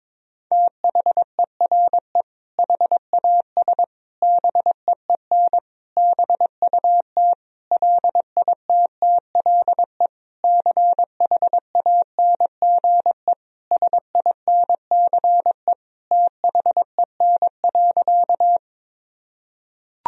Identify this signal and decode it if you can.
Morse code, heard as a keyed tone, about 22 wpm, 700 Hz. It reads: THERE HAS BEEN BUT LITTLE CHANGE SINCE THEN.
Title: Sentences from Top 1000 Words - Review Part 1 - 22wpm